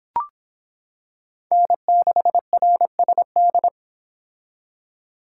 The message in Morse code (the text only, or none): N6RSD